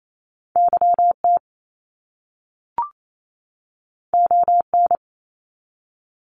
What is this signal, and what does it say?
Morse code, heard as a keyed tone, about 28 wpm, 700 Hz. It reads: YT ON
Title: Canadian Provinces and Territories 28wpm